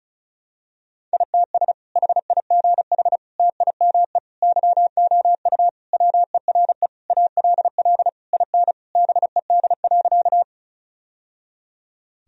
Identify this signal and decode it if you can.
Morse code, heard as a keyed tone, about 36 wpm, 700 Hz. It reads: ITS HIGH TIME YOU WERE ALL IN BED.
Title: Sentences from Top 1000 Words - Review Part 1 - 35wpm